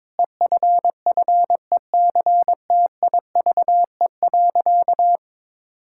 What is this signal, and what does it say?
Morse code, heard as a keyed tone, about 22 wpm, 700 Hz. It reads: EFFECTIVE.
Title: Sentences from Top 1000 Words - Review Part 6 - 22wpm